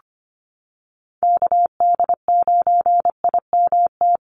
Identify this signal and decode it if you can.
Morse code, heard as a keyed tone, about 25 wpm, 700 Hz. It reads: KD9IMT
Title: USA Call Signs 25wpm